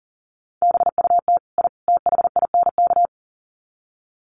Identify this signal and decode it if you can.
Morse code, heard as a keyed tone, about 37 wpm, 700 Hz. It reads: BUT I THINK
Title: Sets of 3 Words 40wpm